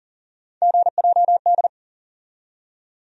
Morse code, heard as a keyed tone, about 40 wpm, 700 Hz. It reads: GJD
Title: USA Call Sign Postfixes - 3 Letters 40wpm